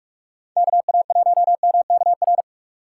Morse code, heard as a keyed tone, about 46 wpm, 700 Hz. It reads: KA1MKR